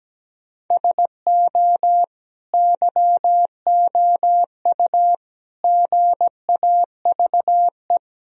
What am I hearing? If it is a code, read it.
Morse code, heard as a keyed tone, about 17 wpm, 700 Hz. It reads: SO YOU GAVE
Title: Sentences from Top 1000 Words - Review Part 2 - 17wpm